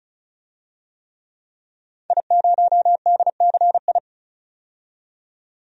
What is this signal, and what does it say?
Morse code, heard as a keyed tone, about 34 wpm, 700 Hz. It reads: I0DCI
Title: DXCC-100 Call Signs - 5 Characters - 35wpm